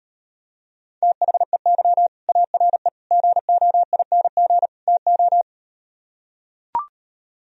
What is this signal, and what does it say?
Morse code, heard as a keyed tone, about 38 wpm, 700 Hz. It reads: THEY ARE GOING TO